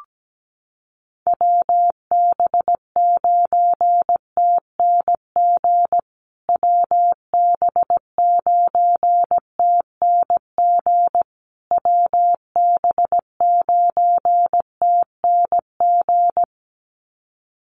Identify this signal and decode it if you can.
Morse code, heard as a keyed tone, about 17 wpm, 700 Hz. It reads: WB9TNG WB9TNG WB9TNG
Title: USA Call Signs Repeated 3x 17wpm